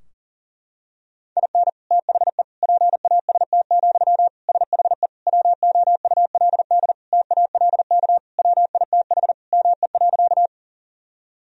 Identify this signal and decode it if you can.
Morse code, heard as a keyed tone, about 40 wpm, 700 Hz. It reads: IN THE PAST, SHE WOULD TALK WITH ME.
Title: Sentences from Top 1000 Words - Review Part 3 - 40wpm